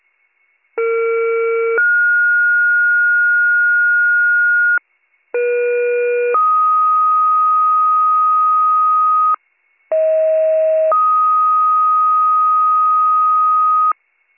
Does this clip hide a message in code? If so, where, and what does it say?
Yes: two-tone page from 0.8 to 13.9 s, heard as a one-second tone, then a three-second tone, about 460 then 1430 Hz, about 480 then 1250 Hz, about 640 then 1250 Hz.